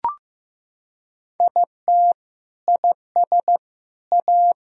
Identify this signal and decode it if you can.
Morse code, heard as a keyed tone, about 16 wpm, 700 Hz. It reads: IT IS A
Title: Sentences from Top 1000 Words - Review Part 4 - 15wpm